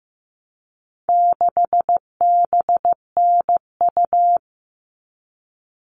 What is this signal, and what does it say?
Morse code, heard as a keyed tone, about 15 wpm, 700 Hz. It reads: 6BNU
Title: All Cut Numbers - 4-Digit 15wpm